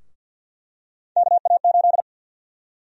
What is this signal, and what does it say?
Morse code, heard as a keyed tone, about 52 wpm, 700 Hz. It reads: KA8
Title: USA Call Sign Prefixes - Repeated 2x - 50wpm